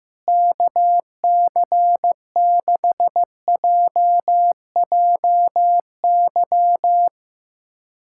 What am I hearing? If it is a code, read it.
Morse code, heard as a keyed tone, about 15 wpm, 700 Hz. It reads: KC6JJY
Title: USA Callsign DE Callsign 15wpm